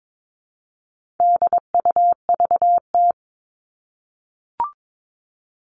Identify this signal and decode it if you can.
Morse code, heard as a keyed tone, about 22 wpm, 700 Hz. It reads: DUVT
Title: All Cut Numbers - 4-Digit 22wpm